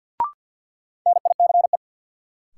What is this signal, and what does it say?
Morse code, heard as a keyed tone, about 53 wpm, 700 Hz. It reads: NICE
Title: Sentences from Top 1100 Words - New Word Drill - 50wpm